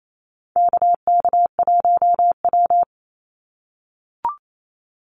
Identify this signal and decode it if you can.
Morse code, heard as a keyed tone, about 28 wpm, 700 Hz. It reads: KK1W
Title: NA SOTA Activator Callsigns 28wpm